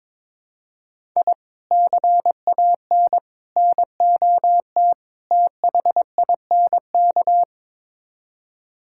Morse code, heard as a keyed tone, about 22 wpm, 700 Hz. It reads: I CAN NOT THINK
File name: practice / 22wpm / Build A Phrase - 5-Word Phrases - 22wpm